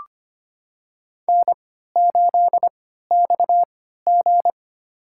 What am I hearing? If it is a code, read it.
Morse code, heard as a keyed tone, about 25 wpm, spaced as for 14 wpm, 700 Hz. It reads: N8XG
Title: USA Call Signs - 4 Characters - 3x Character Spacing - 25wpm